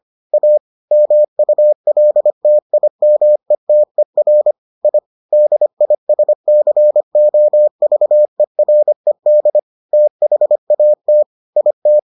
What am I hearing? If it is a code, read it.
Morse code, heard as a keyed tone, about 25 wpm, 600 Hz. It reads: A MULTIMETER I DISCOVERED THAT IT